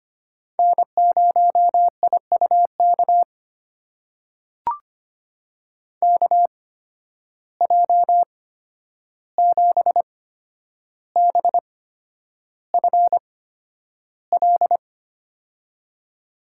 Morse code, heard as a keyed tone, about 25 wpm, spaced as for 10 wpm, 700 Hz. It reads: N0IUK KJ7BFL
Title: USA Call Signs - 8x Character Spacing - 25wpm